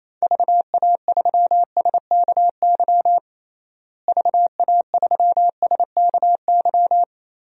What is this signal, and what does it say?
Morse code, heard as a keyed tone, about 28 wpm, 700 Hz. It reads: VA3SKY VA3SKY
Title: DXCC-100 Call Signs - 6 Characters - Repeated 3x - 28wpm